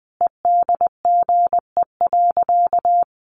Morse code, heard as a keyed tone, about 20 wpm, 700 Hz. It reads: EDGE.
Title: Additional Easy Phrases 20wpm